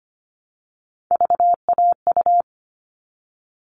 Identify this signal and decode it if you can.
Morse code, heard as a keyed tone, about 25 wpm, 700 Hz. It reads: VAU
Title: All Cut Numbers - 3-Digit 25wpm